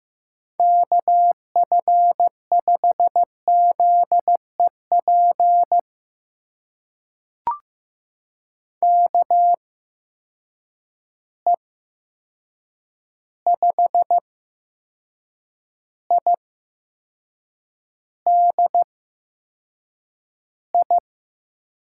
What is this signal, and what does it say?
Morse code, heard as a keyed tone, about 15 wpm, spaced as for 6 wpm, 700 Hz. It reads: KF5ZEP KE5IDI
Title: USA Call Signs - 6 Characters - 8x Character Spacing - 15wpm